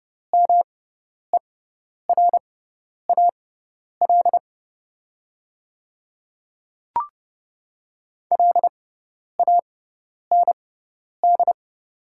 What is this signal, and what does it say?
Morse code, heard as a keyed tone, about 30 wpm, spaced as for 10 wpm, 700 Hz. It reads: MERAL LAND
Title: 6x Character Spacing - Top 500 Words - 30wpm